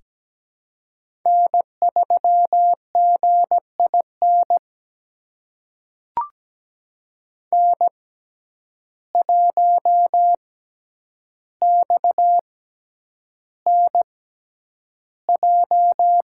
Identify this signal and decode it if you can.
Morse code, heard as a keyed tone, about 17 wpm, spaced as for 9 wpm, 700 Hz. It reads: N3GIN N1XNJ
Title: USA Call Signs - 5 Characters - 6x Character Spacing - 17wpm